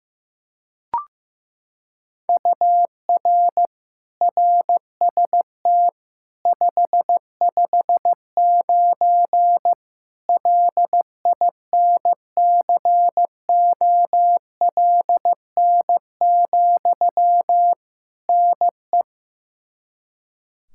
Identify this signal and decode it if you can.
Morse code, heard as a keyed tone, about 15 wpm, 700 Hz. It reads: UR RST 559 LINCOLN, NE
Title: Practice QSO - Exchange 1 - Beginning - 15wpm